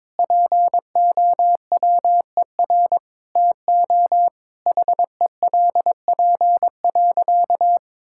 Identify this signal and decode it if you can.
Morse code, heard as a keyed tone, about 22 wpm, 700 Hz. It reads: POWER TO HELP.
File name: practice / 22wpm / Sentences with 8-Word Phrases - 22wpm